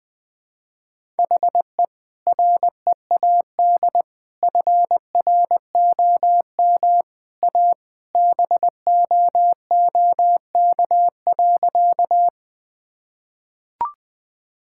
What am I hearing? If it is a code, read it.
Morse code, heard as a keyed tone, about 20 wpm, 700 Hz. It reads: HE READ FROM A BOOK.
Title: Sentences with 2-Word Phrases - 20wpm